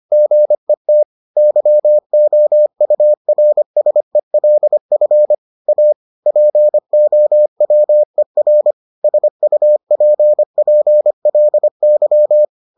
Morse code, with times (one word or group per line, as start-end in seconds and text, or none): 0.1-1.0 GET
1.4-5.3 YOURSELF
5.7-5.9 A
6.3-8.7 POWER
9.0-12.4 SUPPLY